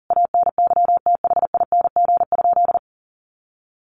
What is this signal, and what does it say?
Morse code, heard as a keyed tone, about 37 wpm, 700 Hz. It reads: ANYTHING?
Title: Sentences from Top 1100 Words 40wpm